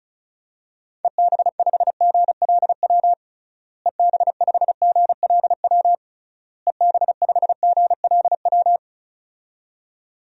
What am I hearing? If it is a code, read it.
Morse code, heard as a keyed tone, about 36 wpm, 700 Hz. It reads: EB5GLW EB5GLW EB5GLW
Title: DXCC-100 Call Signs - 6 Characters - Repeated 3x - 35wpm